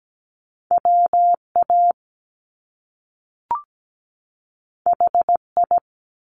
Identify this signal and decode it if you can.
Morse code, heard as a keyed tone, about 17 wpm, 700 Hz. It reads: WA HI
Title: US States 17wpm